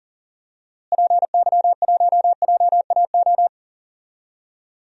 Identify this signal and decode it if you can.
Morse code, heard as a keyed tone, about 39 wpm, 700 Hz. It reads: PY1JAO
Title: DXCC-100 Call Signs - 6 Characters - Repeated 2x - 40wpm